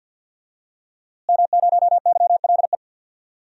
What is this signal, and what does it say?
Morse code, heard as a keyed tone, about 50 wpm, 700 Hz. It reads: M0YLE
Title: DXCC-100 Call Signs - 5 Characters - Repeated 2x - 50wpm